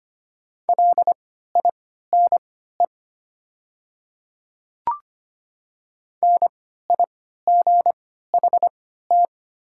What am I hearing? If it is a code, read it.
Morse code, heard as a keyed tone, about 25 wpm, spaced as for 14 wpm, 700 Hz. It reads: LINE NIGHT
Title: 3x Character Spacing - Top 500 Words - 25wpm